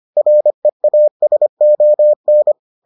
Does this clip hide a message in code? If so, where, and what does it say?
Yes: Morse code from 0.2 to 2.5 s, heard as a keyed tone, about 25 wpm, 600 Hz. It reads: REASON